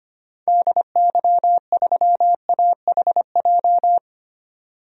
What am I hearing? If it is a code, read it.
Morse code, heard as a keyed tone, about 25 wpm, 700 Hz. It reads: DY3AHJ
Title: DXCC-100 Call Signs - 6 Characters - 25wpm